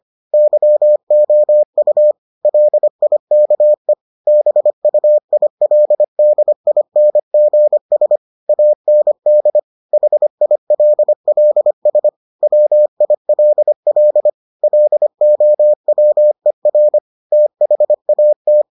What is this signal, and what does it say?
Morse code, heard as a keyed tone, about 25 wpm, 600 Hz. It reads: YOU LIKE BUILDINGS AND HILLS WILL LOWER THAT